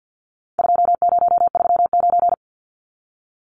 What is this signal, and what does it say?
Morse code, heard as a keyed tone, about 50 wpm, 700 Hz. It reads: 2039
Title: Numbers - 4-Digit 50wpm